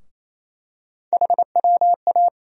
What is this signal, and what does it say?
Morse code, heard as a keyed tone, about 28 wpm, 700 Hz. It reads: HWA
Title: USA Call Sign Postfixes - 3 Letters 28wpm